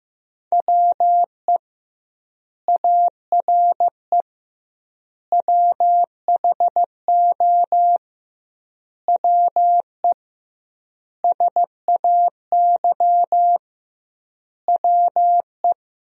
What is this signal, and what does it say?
Morse code, heard as a keyed tone, about 15 wpm, 700 Hz. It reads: WE ARE WHO WE SAY WE
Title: 2x Word-Spacing - Sentences from Top 200 Words - 15wpm